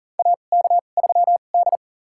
Morse code, heard as a keyed tone, about 40 wpm, spaced as for 29 wpm, 700 Hz. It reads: AK3D
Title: USA Call Signs - 4 Characters - 2x Character Spacing - 40wpm